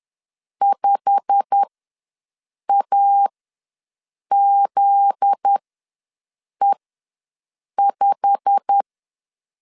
Morse code, heard as a keyed tone, about 10 wpm, spaced as for 6 wpm, 800 Hz. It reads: 5AZE5
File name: LEZIONI / 8 YJ45 / 8 YJ45 ws 10wpm 3843